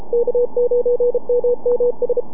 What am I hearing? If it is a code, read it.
Morse code, heard as a keyed tone, about 33 wpm, 500 Hz, with radio noise. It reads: K9MMS
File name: NAQPCW-8-2009 / W1UJ NAQP CW 09 1800Z